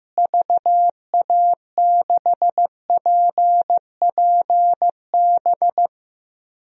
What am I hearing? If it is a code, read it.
Morse code, heard as a keyed tone, about 16 wpm, 700 Hz. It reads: VA6PPB
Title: DXCC-100 Call Signs - 6 Characters - Repeated 2x - 15wpm